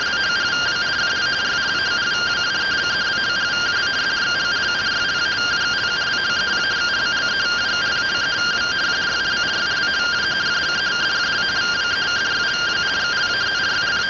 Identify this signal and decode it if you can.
Two-tone teleprinter signal, about 170 Hz shift.